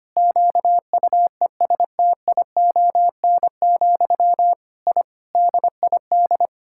Morse code, heard as a keyed tone, about 25 wpm, 700 Hz. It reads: QUESTION, I DID
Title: Sentences from Top 1000 Words - Review Part 6 - 25wpm